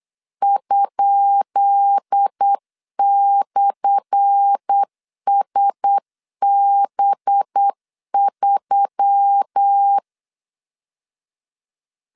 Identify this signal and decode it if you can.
Morse code, heard as a keyed tone, about 8 wpm, 800 Hz. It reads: ?/SB3
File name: LEZIONI / 10 SIGNS / 10B INT 8wpm 3534